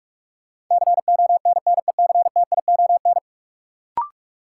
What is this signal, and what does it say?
Morse code, heard as a keyed tone, about 41 wpm, 700 Hz. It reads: CONNECTION
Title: Sentences from Top 1300 Words - New Word Drill - 45wpm